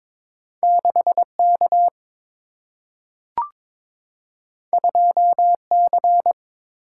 Morse code, heard as a keyed tone, about 22 wpm, 700 Hz. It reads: 6 K 2 C
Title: Single-Letter Single-Number 22wpm